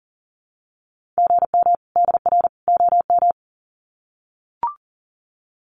Morse code, heard as a keyed tone, about 38 wpm, 700 Hz. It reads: GM DR OM
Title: Common QSO Phrases 40wpm